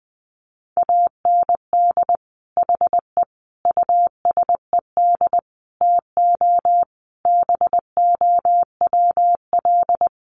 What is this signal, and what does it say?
Morse code, heard as a keyed tone, about 20 wpm, 700 Hz. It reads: AND HE USED TO BOWL
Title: Sentences from Top 1200 Words 20wpm